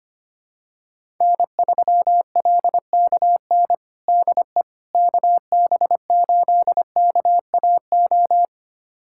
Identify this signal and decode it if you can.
Morse code, heard as a keyed tone, about 25 wpm, 700 Hz. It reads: N3LKN DE KB8KAO
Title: USA Callsign DE Callsign 25wpm